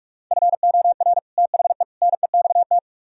Morse code, heard as a keyed tone, about 49 wpm, 700 Hz. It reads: FOR THE NEXT